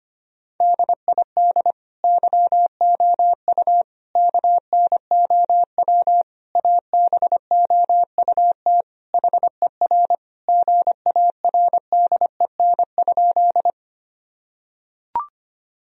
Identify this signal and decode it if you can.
Morse code, heard as a keyed tone, about 25 wpm, 700 Hz. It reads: DID YOU KNOW ABOUT HER GARDEN?
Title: Sentences from Top 1000 Words - Review Part 4 - 25wpm